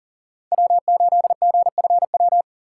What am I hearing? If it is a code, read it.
Morse code, heard as a keyed tone, about 40 wpm, 700 Hz. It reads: W8GFW
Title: USA Call Signs - 5 Characters - 3x Character Spacing - 40wpm